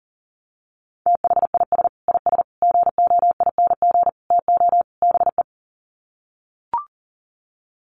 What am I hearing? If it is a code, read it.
Morse code, heard as a keyed tone, about 40 wpm, 700 Hz. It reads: THIS IS GOING TO BE
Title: Sets of 5 Words 40wpm